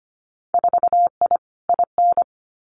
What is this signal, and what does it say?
Morse code, heard as a keyed tone, about 25 wpm, 700 Hz. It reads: 4I IN
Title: Winter Field Day Exchanges 25wpm